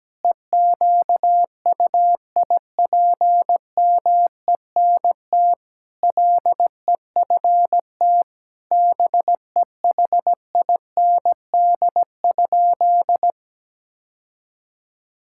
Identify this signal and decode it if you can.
Morse code, heard as a keyed tone, about 17 wpm, 700 Hz. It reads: EQUIPMENT LEFT BEHIND?
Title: Sentences from Top 1000 Words - Review Part 5 - 17wpm